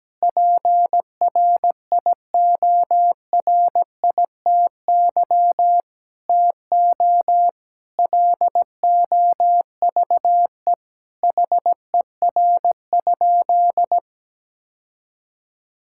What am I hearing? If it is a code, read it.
Morse code, heard as a keyed tone, about 17 wpm, 700 Hz. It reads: PRIORITY TO LOVE HER?
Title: Sentences from Top 1600 Words 17wpm